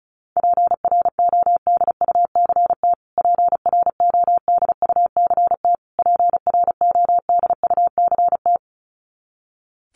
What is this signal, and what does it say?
Morse code, heard as a keyed tone, about 34 wpm, 700 Hz. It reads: PRODUCT PRODUCT PRODUCT
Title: Top 500 Words Repeated 35wpm